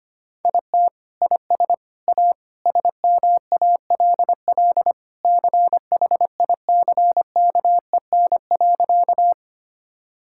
Morse code, heard as a keyed tone, about 25 wpm, 700 Hz. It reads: IT IS A SMALL CHICKEN.
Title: Sentences from Top 1000 Words - Review Part 5 - 25wpm